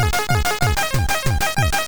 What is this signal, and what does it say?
There is no code.